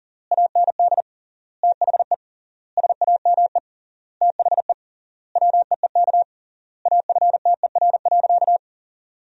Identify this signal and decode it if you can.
Morse code, heard as a keyed tone, about 40 wpm, 700 Hz. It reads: AND THE SAME THE WEEK AFTER.
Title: 3x Word-Spacing - Sentences from Top 500 Words - 40wpm